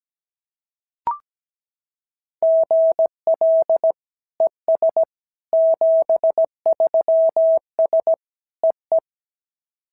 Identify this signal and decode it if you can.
Morse code, heard as a keyed tone, about 16 wpm, 650 Hz. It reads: GL ES 73S EE
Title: SOTA QSO Practice 17wpm